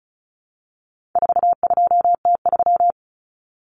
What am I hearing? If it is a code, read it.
Morse code, heard as a keyed tone, about 34 wpm, 700 Hz. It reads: 42T3
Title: Common Cut Numbers - 4-Digit 35wpm